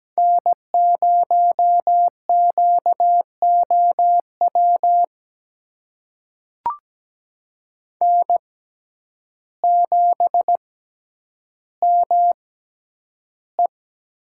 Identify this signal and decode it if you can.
Morse code, heard as a keyed tone, about 17 wpm, spaced as for 9 wpm, 700 Hz. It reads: N0QOW N7ME
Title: USA Call Signs - 5 Characters - 6x Character Spacing - 17wpm